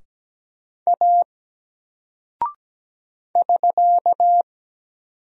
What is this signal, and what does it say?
Morse code, heard as a keyed tone, about 17 wpm, 700 Hz. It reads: A <SK>
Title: Single Character (Letter:Number:Punctuation:Prosign) 17wpm